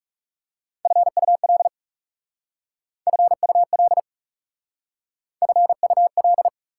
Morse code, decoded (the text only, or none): FUL FUL FUL